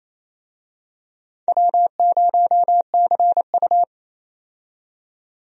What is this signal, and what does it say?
Morse code, heard as a keyed tone, about 28 wpm, 700 Hz. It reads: W0CU
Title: USA Call Signs - 4 Characters - 28wpm